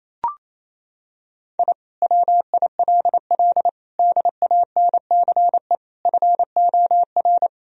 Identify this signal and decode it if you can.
Morse code, heard as a keyed tone, about 28 wpm, 700 Hz. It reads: I WILL DANCE FOR
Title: Sentences from Top 1000 Words - Review Part 4 - 28wpm